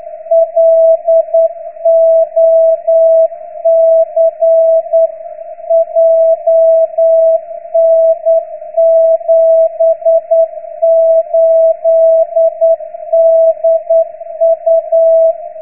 Morse code, heard as a keyed tone, about 9 wpm, 660 Hz. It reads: LOC JN78DU